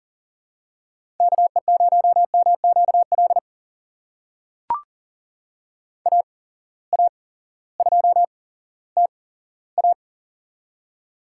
Morse code, heard as a keyed tone, about 38 wpm, spaced as for 18 wpm, 700 Hz. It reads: KE0MQL AA2TA
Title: USA Call Signs - 8x Character Spacing - 40wpm